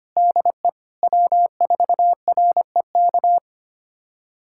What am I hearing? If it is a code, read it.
Morse code, heard as a keyed tone, about 25 wpm, 700 Hz. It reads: DE W4REK
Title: USA Callsign DE Callsign 25wpm